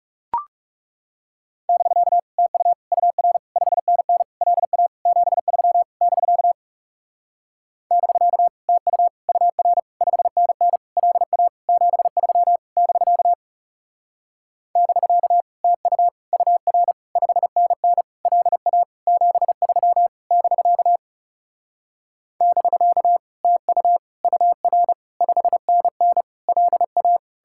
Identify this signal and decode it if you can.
Morse code, heard as a keyed tone, about 40 wpm, 700 Hz. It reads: <BK> TU UR 5NN LA 73 <BK> <BK> TU UR 5NN LA 73 <BK> <BK> TU UR 5NN LA 73 <BK> <BK> TU UR 5NN LA